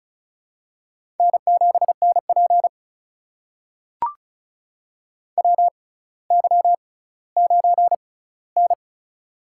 Morse code, heard as a keyed tone, about 34 wpm, spaced as for 18 wpm, 700 Hz. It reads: N7NP WY9N
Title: USA Call Signs - 4 Characters - 6x Character Spacing - 35wpm